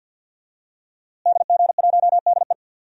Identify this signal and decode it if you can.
Morse code, heard as a keyed tone, about 52 wpm, 700 Hz. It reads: DG1DE
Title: DXCC-100 Call Signs - 5 Characters - Repeated 3x - 50wpm